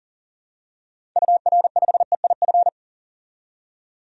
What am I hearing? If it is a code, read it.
Morse code, heard as a keyed tone, about 38 wpm, 700 Hz. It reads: UR5EIF
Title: DXCC-100 Call Signs - 6 Characters - 40wpm